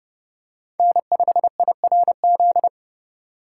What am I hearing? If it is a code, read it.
Morse code, heard as a keyed tone, about 30 wpm, 700 Hz. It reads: N5IRZ
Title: USA Call Signs - 5 Characters - 4x Character Spacing - 30wpm